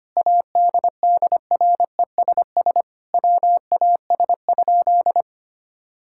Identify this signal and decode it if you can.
Morse code, heard as a keyed tone, about 25 wpm, 700 Hz. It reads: ADDRESS WAS?
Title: Sentences from Top 1000 Words - Review Part 6 - 25wpm